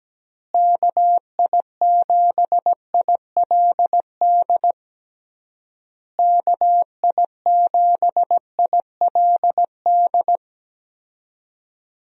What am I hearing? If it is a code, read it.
Morse code, heard as a keyed tone, about 17 wpm, 700 Hz. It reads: KI7ILD KI7ILD
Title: USA Call Signs - 6 Characters - Repeated 3x - 17wpm